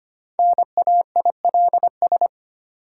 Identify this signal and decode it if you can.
Morse code, heard as a keyed tone, about 25 wpm, 700 Hz. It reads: NAILS
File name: practice / 25wpm / Easy Words with Hints - 25wpm